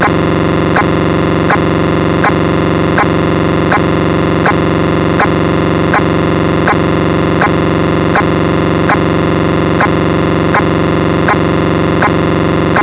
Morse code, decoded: none